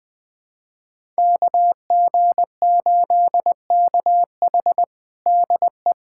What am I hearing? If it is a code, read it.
Morse code, heard as a keyed tone, about 20 wpm, 700 Hz. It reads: KG8KH DE
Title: USA Callsign DE Callsign - 5 Characters - 20wpm